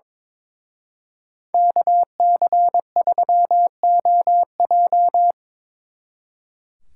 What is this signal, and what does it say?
Morse code, heard as a keyed tone, about 25 wpm, 700 Hz. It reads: KC3OJ
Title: Speed Racing - USA 2x2 Call Signs 22wpm